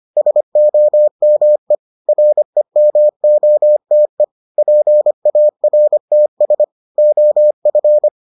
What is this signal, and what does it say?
Morse code, heard as a keyed tone, about 25 wpm, 600 Hz. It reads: SOME REMOTE PARTS OF